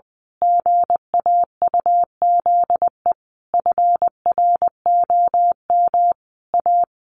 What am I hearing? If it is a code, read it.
Morse code, heard as a keyed tone, about 20 wpm, 700 Hz. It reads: GAUZE FROM A
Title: A Sound Of Thunder 20wpm